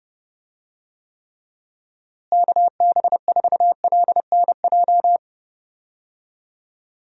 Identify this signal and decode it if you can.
Morse code, heard as a keyed tone, about 32 wpm, 700 Hz. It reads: KB4LNJ
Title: USA Call Signs - 6 Characters - Repeated 3x - 30wpm